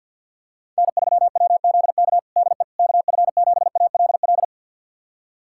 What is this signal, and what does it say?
Morse code, heard as a keyed tone, about 51 wpm, 700 Hz. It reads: N3WZK DE KF7ALL
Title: USA Callsign DE Callsign 50wpm